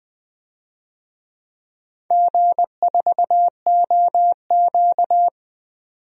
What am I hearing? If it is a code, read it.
Morse code, heard as a keyed tone, about 20 wpm, 700 Hz. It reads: G4OQ
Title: DXCC-100 Call Signs - 4 Characters - 20wpm